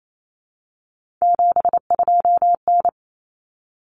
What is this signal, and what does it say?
Morse code, heard as a keyed tone, about 28 wpm, 700 Hz. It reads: 72N
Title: Common Cut Numbers - 3-Digit 28wpm